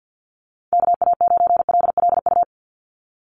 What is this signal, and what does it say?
Morse code, heard as a keyed tone, about 50 wpm, 700 Hz. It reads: KA9LLU
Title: USA Call Signs 50wpm